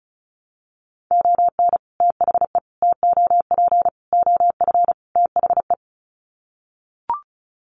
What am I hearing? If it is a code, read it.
Morse code, heard as a keyed tone, about 34 wpm, 700 Hz. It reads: ON THE TOP OF THE
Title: Sets of 5 Words Encore 35wpm